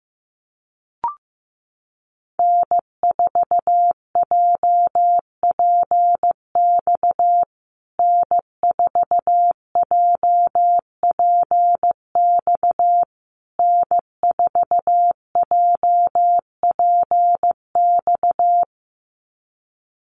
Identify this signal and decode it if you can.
Morse code, heard as a keyed tone, about 15 wpm, 700 Hz. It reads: N4JPX N4JPX N4JPX
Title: USA Call Signs Repeated 3x 15wpm